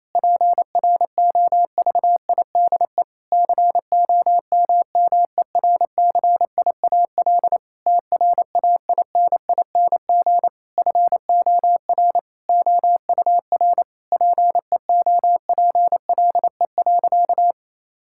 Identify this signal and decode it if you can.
Morse code, heard as a keyed tone, about 28 wpm, 700 Hz. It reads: PROVIDE COMMERCIAL TRAINING FOR OUR PEOPLE.
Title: Sentences from Top 1000 Words - Review Part 5 - 28wpm